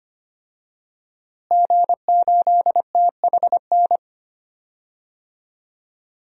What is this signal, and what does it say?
Morse code, heard as a keyed tone, about 25 wpm, 700 Hz. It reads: G8THN
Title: DXCC-100 Call Signs - 5 Characters - Repeated 2x - 25wpm